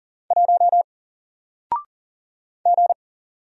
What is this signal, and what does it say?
Morse code, heard as a keyed tone, about 39 wpm, 700 Hz. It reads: 1 G
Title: Single Letter-Number 40wpm